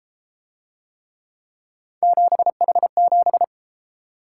Morse code, heard as a keyed tone, about 31 wpm, 700 Hz. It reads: 7H7